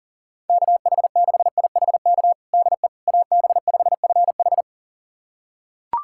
Morse code, heard as a keyed tone, about 37 wpm, 700 Hz. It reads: KH6IHK DE AB5FH
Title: USA Callsign DE Callsign 40wpm